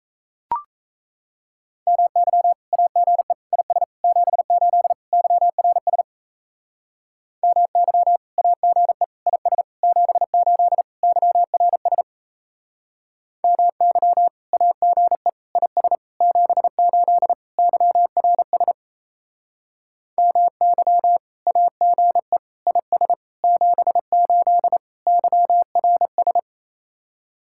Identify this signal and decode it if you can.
Morse code, heard as a keyed tone, about 36 wpm, 700 Hz. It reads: MY AGE IS 78 YRS MY AGE IS 78 YRS MY AGE IS 78 YRS MY AGE IS 78 YRS